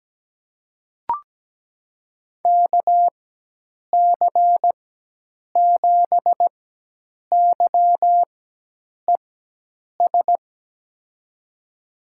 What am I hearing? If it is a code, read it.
Morse code, heard as a keyed tone, about 17 wpm, spaced as for 8 wpm, 700 Hz. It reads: KC7YES
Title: USA Call Signs - 6 Characters - 4x Character Spacing - 17wpm